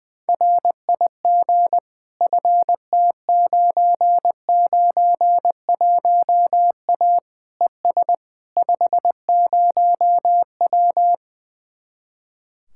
Morse code, heard as a keyed tone, about 20 wpm, 700 Hz. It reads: RIG FT991A ES 50W
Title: Speed-Racing Common QSO Phrases - 20wpm